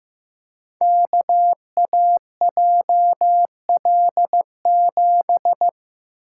Morse code, heard as a keyed tone, about 15 wpm, 700 Hz. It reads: KAJL7
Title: Code Groups - 5 Letter:Numbers - 15wpm